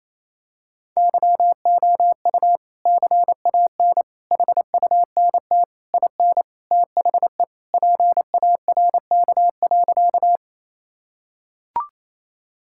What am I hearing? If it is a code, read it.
Morse code, heard as a keyed tone, about 27 wpm, 700 Hz. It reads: YOU CAN HUNT IN THE PARK.
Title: Sentences from Top 1600 Words 28wpm